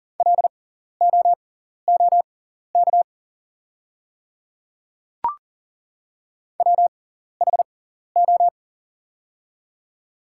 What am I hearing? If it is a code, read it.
Morse code, heard as a keyed tone, about 40 wpm, spaced as for 14 wpm, 690 Hz. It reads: LOOK WHO
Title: 6x Character Spacing - Top 100 Words - 40wpm